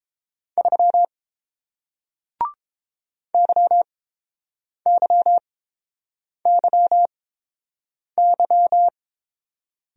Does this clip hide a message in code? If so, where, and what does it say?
Yes: Morse code from 0.6 to 8.9 s, heard as a keyed tone, about 30 wpm, spaced as for 7 wpm, 700 Hz. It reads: 3 YYYY